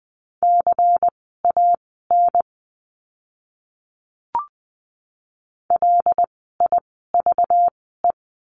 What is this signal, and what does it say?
Morse code, heard as a keyed tone, about 20 wpm, spaced as for 15 wpm, 700 Hz. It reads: CAN LIVE
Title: 244 Lesson 39f - Letter z - Words 20wpm